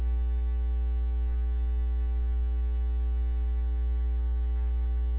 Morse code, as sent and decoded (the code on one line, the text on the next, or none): none